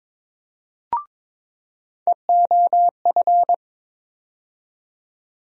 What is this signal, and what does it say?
Morse code, heard as a keyed tone, about 22 wpm, 700 Hz. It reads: EOF
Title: USA Call Sign Postfixes - 3 Letters 22wpm